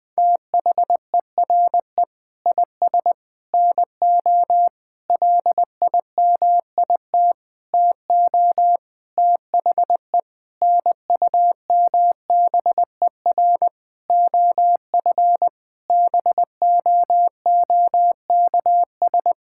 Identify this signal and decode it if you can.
Morse code, heard as a keyed tone, about 20 wpm, 700 Hz. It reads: THERE IS NO LIMIT TO THE NUMBER OF BOOKS
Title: Sentences with 8-Word Phrases - 20wpm